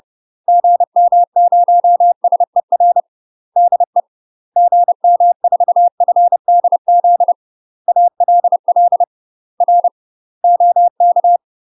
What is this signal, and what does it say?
Morse code, heard as a keyed tone, about 30 wpm, 700 Hz. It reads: GM0SER DE GM4FDZ ALL R OK
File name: UK MASTER QSOs TRAINING / 30WPM / UK MASTER QSOs 0046